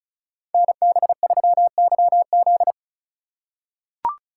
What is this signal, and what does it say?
Morse code, heard as a keyed tone, about 34 wpm, 700 Hz. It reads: NB3YZ
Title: Code Groups - 5 Letter:Numbers - 35wpm